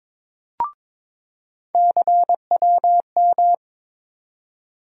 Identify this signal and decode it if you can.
Morse code, heard as a keyed tone, about 22 wpm, 700 Hz. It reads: CWM
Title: USA Call Sign Postfixes - 3 Letters 22wpm